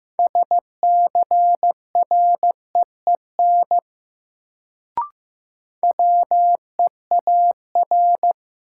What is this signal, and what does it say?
Morse code, heard as a keyed tone, about 15 wpm, 700 Hz. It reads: SCREEN WEAR
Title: Sentences from Top 1000 Words - Review Part 6 - New Word Drill - 15wpm